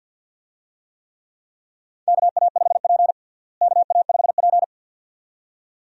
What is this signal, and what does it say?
Morse code, heard as a keyed tone, about 53 wpm, 700 Hz. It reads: KA5P KA5P
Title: USA Call Signs - 4 Characters - Repeated 2x - 50wpm